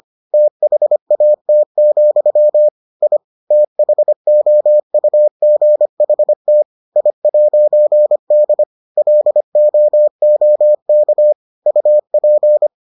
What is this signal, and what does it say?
Morse code, heard as a keyed tone, about 25 wpm, 600 Hz. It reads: THAT, I THOUGHT I'D LOOK UP